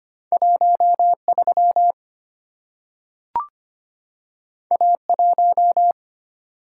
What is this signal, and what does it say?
Morse code, heard as a keyed tone, about 25 wpm, 700 Hz. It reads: 13 A1
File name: Two Letter-Number Combinations - 25wpm